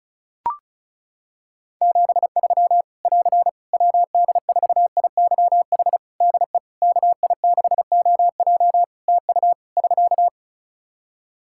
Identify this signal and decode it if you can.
Morse code, heard as a keyed tone, about 34 wpm, 700 Hz. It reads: 73 <AR> WD4IYH DE KI6OJ TU <SK>
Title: K1USN SST QSO Practice 35wpm